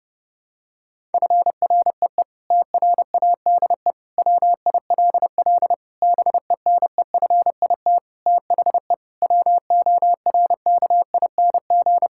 Morse code, heard as a keyed tone, about 30 wpm, 700 Hz. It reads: FREE TRADE WILL BENEFIT THE WORKING
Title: Sentences from Top 1000 Words - Review Part 6 - 30wpm